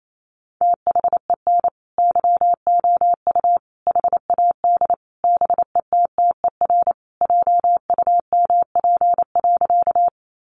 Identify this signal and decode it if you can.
Morse code, heard as a keyed tone, about 28 wpm, 700 Hz. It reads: THEN YOU HAD BETTER JUMP.
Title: Sentences from Top 1300 Words 28wpm